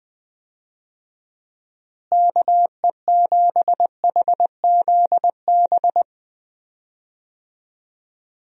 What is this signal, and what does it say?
Morse code, heard as a keyed tone, about 20 wpm, 700 Hz. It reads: KE7HZB
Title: USA Call Signs - 6 Characters - Repeated 2x - 20wpm